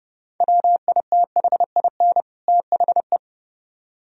Morse code, heard as a keyed tone, about 37 wpm, 700 Hz. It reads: WITHIN THE
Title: Speed-Racing - Sets of 2 Words 30wpm